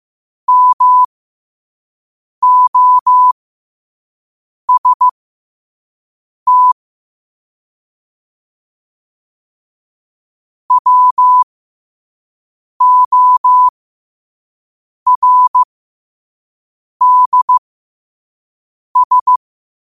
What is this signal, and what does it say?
Morse code, heard as a keyed tone, about 15 wpm, spaced as for 5 wpm, 1000 Hz. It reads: MOST WORDS
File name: 5wpm